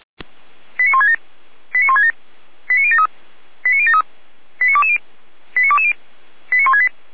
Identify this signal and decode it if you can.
Five-tone selective call (ZVEI), heard as quick runs of five tones, tones of about 70 ms. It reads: 89167 89167 89073 89073 89220 89220 89168
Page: SDR/ZVEI